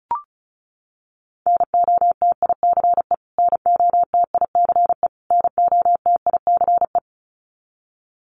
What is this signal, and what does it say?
Morse code, heard as a keyed tone, about 34 wpm, 700 Hz. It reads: NOTICE NOTICE NOTICE
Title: Top 500 Words Repeated 35wpm